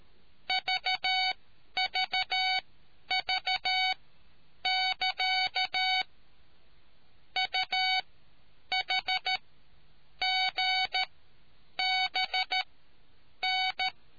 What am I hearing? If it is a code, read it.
Morse code, heard as a keyed tone, about 13 wpm, spaced as for 8 wpm, 750 Hz. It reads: VVV<KA> UHGBN